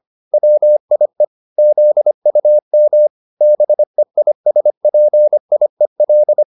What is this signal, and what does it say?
Morse code, heard as a keyed tone, about 25 wpm, 600 Hz. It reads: WIE ZUM BEISPIEL